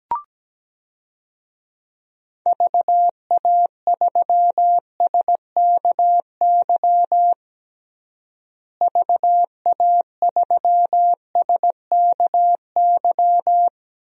Morse code, heard as a keyed tone, about 17 wpm, 700 Hz. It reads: VA3SKY VA3SKY
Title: DXCC-100 Call Signs - 6 Characters - Repeated 3x - 17wpm